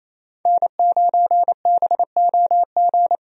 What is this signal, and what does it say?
Morse code, heard as a keyed tone, about 28 wpm, 700 Hz. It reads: N9BOG
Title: USA Call Signs - 5 Characters - Repeated 3x - 28wpm